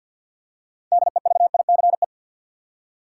Morse code, heard as a keyed tone, about 50 wpm, 700 Hz. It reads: DEVICE
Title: Six Letter Words - 50wpm